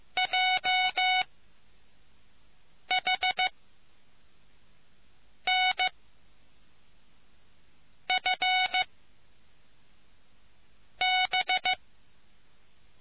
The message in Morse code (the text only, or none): J H N F B